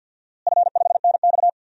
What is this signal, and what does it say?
Morse code, heard as a keyed tone, about 52 wpm, 700 Hz. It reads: F5NX
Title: DXCC-100 Call Signs - 4 Characters - Repeated 2x - 50wpm